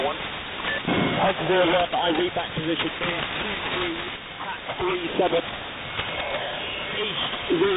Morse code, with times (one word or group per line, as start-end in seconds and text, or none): none